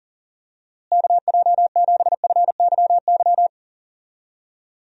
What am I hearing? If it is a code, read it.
Morse code, heard as a keyed tone, about 40 wpm, 700 Hz. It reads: KJ7FYY
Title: USA Call Signs - 6 Characters - Repeated 3x - 40wpm